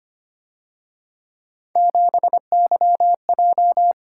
Morse code, heard as a keyed tone, about 29 wpm, 700 Hz. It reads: 7YJ
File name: Speed-Racing - Three Letter-Number Combinations - 25wpm